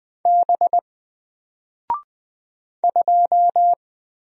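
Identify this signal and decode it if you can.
Morse code, heard as a keyed tone, about 20 wpm, 700 Hz. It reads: B 2
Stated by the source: Single letter and number Morse drill at 20 wpm, from N4LCD.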